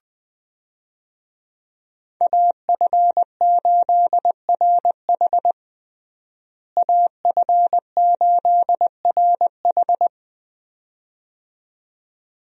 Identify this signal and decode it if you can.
Morse code, heard as a keyed tone, about 20 wpm, 700 Hz. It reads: AF8RH AF8RH
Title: USA Call Signs - 5 Characters - Repeated 2x - 20wpm